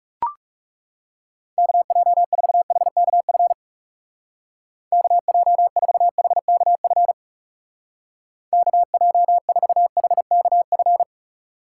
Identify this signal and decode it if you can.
Morse code, heard as a keyed tone, about 36 wpm, 700 Hz. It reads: KJ4HKF KJ4HKF KJ4HKF
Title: Speed Racing - USA 2x3 Call Signs 30wpm